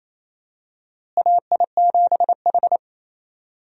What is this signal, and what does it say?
Morse code, heard as a keyed tone, about 28 wpm, 700 Hz. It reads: AI7H
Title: USA Call Signs - 4 Characters - Repeated 2x - 28wpm